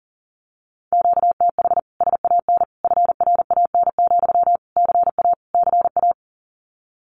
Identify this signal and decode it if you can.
Morse code, heard as a keyed tone, about 40 wpm, 700 Hz. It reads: QTH SAN FRAN, CA CA
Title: Common QSO Phrases Encore 40wpm